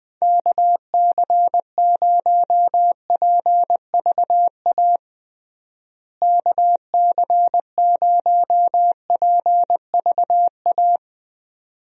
Morse code, heard as a keyed tone, about 21 wpm, 700 Hz. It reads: KC0PVA KC0PVA
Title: USA Call Signs - 6 Characters - Repeated 2x - 20wpm